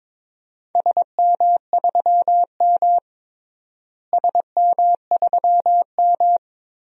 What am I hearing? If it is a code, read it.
Morse code, heard as a keyed tone, about 22 wpm, 700 Hz. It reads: SM3M SM3M
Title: DXCC-100 Call Signs - 4 Characters - Repeated 3x - 22wpm